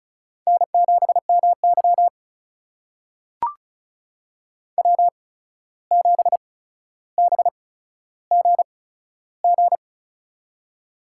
Morse code, heard as a keyed tone, about 36 wpm, spaced as for 14 wpm, 700 Hz. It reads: N7MY W7BGG
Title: USA Call Signs - 8x Character Spacing - 35wpm